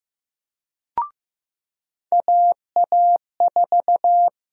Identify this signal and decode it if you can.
Morse code, heard as a keyed tone, about 15 wpm, 700 Hz. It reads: AA4
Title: USA Call Sign Prefixes - 15wpm